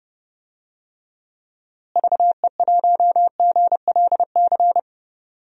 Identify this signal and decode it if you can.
Morse code, heard as a keyed tone, about 31 wpm, 700 Hz. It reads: VE1GLC
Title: DXCC-100 Call Signs - 6 Characters - Repeated 2x - 30wpm